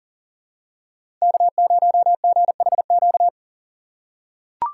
Morse code, heard as a keyed tone, about 40 wpm, 700 Hz. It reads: K0GHQ
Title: USA Call Signs - 5 Characters - 4x Character Spacing - 40wpm